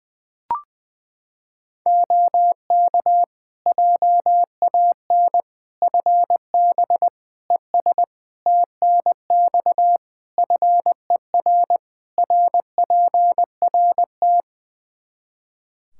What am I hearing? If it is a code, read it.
Morse code, heard as a keyed tone, about 20 wpm, 700 Hz. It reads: OK JAN FB ES TNX FER RPRT
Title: K1USN SST QSO Practice 20wpm